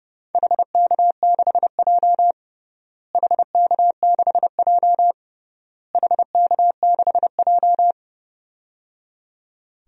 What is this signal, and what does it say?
Morse code, heard as a keyed tone, about 30 wpm, 700 Hz. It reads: HK6J HK6J HK6J
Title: DXCC-100 Call Signs - 4 Characters - Repeated 3x - 30wpm